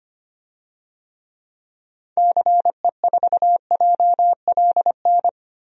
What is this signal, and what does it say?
Morse code, heard as a keyed tone, about 25 wpm, 700 Hz. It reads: CE4JLN
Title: DXCC-100 Call Signs - 6 Characters - Repeated 3x - 25wpm